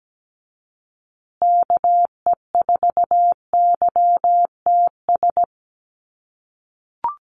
Morse code, heard as a keyed tone, about 17 wpm, 700 Hz. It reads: KE4YTS
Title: USA Call Signs Repeated 3x 17wpm